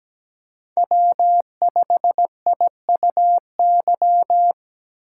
Morse code, heard as a keyed tone, about 20 wpm, 700 Hz. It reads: W5IUY
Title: Speed Racing - USA 1x3 Call Signs 17wpm